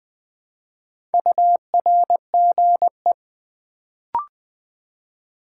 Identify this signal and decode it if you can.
Morse code, heard as a keyed tone, about 20 wpm, 700 Hz. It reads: URGE
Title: Build A Word - 2 Letters At A Time - 4-Letter Words - 20wpm